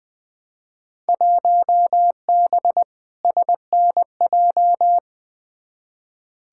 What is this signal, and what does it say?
Morse code, heard as a keyed tone, about 20 wpm, 700 Hz. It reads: 1B SNJ
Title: Field Day Exchanges - 20wpm